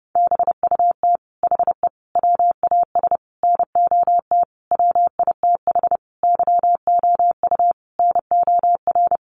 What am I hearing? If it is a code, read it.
Morse code, heard as a keyed tone, about 30 wpm, 700 Hz. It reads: BUT HE WAS NOT WITH YOU NOR
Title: Sentences from Top 1500 Words 30wpm